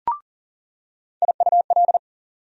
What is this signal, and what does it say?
Morse code, heard as a keyed tone, about 40 wpm, 700 Hz. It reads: IUL
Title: USA Call Sign Postfixes - 3 Letters 40wpm